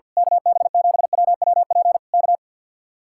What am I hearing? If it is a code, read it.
Morse code, heard as a keyed tone, about 49 wpm, 700 Hz. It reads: KB7WWP K
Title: Speed-Racing Common QSO Phrases - 50wpm